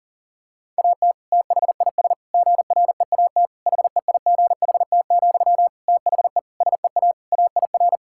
Morse code, heard as a keyed tone, about 40 wpm, 700 Hz. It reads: AT THIS GREAT HEIGHT, THE SEA AIR
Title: Sentences from Top 1600 Words 40wpm